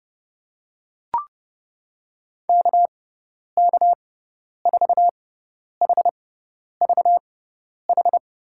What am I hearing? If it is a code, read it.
Morse code, heard as a keyed tone, about 30 wpm, spaced as for 10 wpm, 700 Hz. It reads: KK4HVH
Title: USA Call Signs - 6x Character Spacing - 30wpm